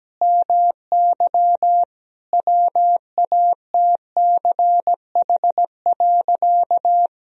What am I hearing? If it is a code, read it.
Morse code, heard as a keyed tone, about 17 wpm, 700 Hz. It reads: MY WATCH.
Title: Sentences from Top 1000 Words - Review Part 1 - 17wpm